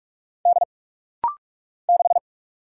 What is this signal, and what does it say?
Morse code, heard as a keyed tone, about 44 wpm, 690 Hz. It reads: D 6
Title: Single Letter-Number - Rapid-Fire 45wpm